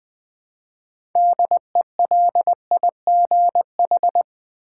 Morse code, heard as a keyed tone, about 20 wpm, 700 Hz. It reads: DELIGH
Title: Build A Word - 3 Letters At A Time - 9-Letter Words - 20wpm